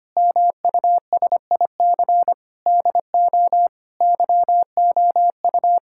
Morse code, heard as a keyed tone, about 25 wpm, 700 Hz. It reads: MUSIC DO YOU
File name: practice / 25wpm / Sentences with 3-Word Phrases - 25wpm